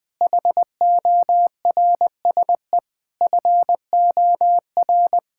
Morse code, heard as a keyed tone, about 20 wpm, 700 Hz. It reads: HORSE FOR
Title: Sentences from Top 1000 Words - Review Part 5 - 20wpm